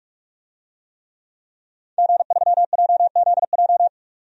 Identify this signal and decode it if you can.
Morse code, heard as a keyed tone, about 46 wpm, 700 Hz. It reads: G3JZJ
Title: DXCC-100 Call Signs - 5 Characters - 45wpm